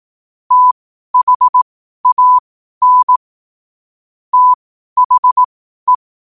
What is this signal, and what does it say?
Morse code, heard as a keyed tone, about 18 wpm, spaced as for 12 wpm, 1000 Hz. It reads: THAN THE
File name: qotdfiles / 12wpm